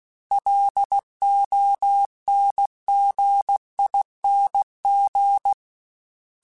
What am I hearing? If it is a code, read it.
Morse code, heard as a keyed tone, about 16 wpm, 800 Hz. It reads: LONGING
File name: Special INGs / Special INGs-mp3s / Sngl ing @15